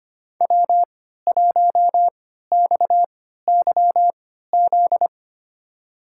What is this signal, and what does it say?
Morse code, heard as a keyed tone, about 25 wpm, spaced as for 14 wpm, 700 Hz. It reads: W1XYZ
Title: USA Call Signs - 5 Characters - 3x Character Spacing - 25wpm